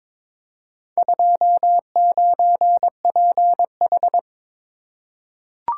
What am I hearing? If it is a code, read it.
Morse code, heard as a keyed tone, about 23 wpm, 700 Hz. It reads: 29PH
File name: practice / 22wpm / Four Letter-Number Combinations - 22wpm